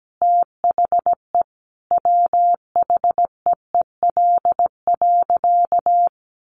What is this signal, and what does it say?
Morse code, heard as a keyed tone, about 17 wpm, 700 Hz. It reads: THE WHEEL.
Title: Sentences from Top 500 Words 17wpm